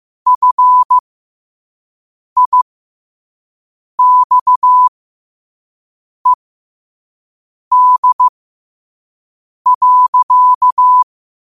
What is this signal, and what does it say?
Morse code, heard as a keyed tone, about 15 wpm, spaced as for 5 wpm, 1000 Hz. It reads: FIXED.
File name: qotdfiles / 5wpm